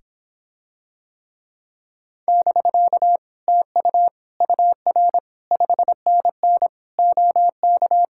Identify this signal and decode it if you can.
Morse code, heard as a keyed tone, about 26 wpm, 700 Hz. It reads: <BK> TU UR 5NN OK